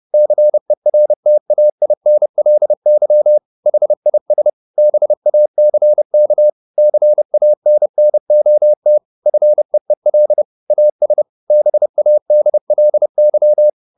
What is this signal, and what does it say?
Morse code, heard as a keyed tone, about 30 wpm, 600 Hz. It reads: CERTAINLY HIS BACK CANNOT FEEL AS BADLY
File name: the old man and the sea / the old man and the sea 1 30wpm